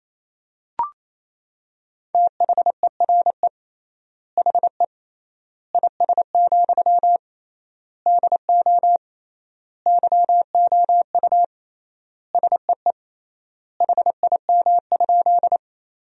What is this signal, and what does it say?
Morse code, heard as a keyed tone, about 28 wpm, 700 Hz. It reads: THERE HE IS, DO YOU SEE HIM?
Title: 3x Word-Spacing - Sentences from Top 100 Words - 28wpm